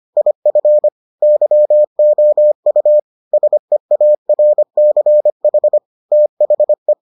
Morse code, heard as a keyed tone, about 25 wpm, 600 Hz. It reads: IF YOU SEARCH THE